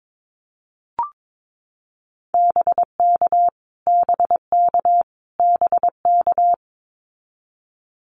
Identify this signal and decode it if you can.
Morse code, heard as a keyed tone, about 22 wpm, 700 Hz. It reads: BK BK BK
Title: Abbreviations Repeated 3x - 22wpm